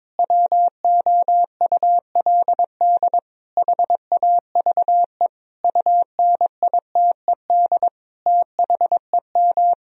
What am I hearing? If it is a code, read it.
Morse code, heard as a keyed tone, about 21 wpm, 700 Hz. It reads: WOULD HAVE UNITED THEM
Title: Sentences from Top 1000 Words - Review Part 4 - 22wpm